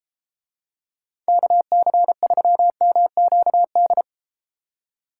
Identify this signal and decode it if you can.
Morse code, heard as a keyed tone, about 32 wpm, 700 Hz. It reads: KC3MQD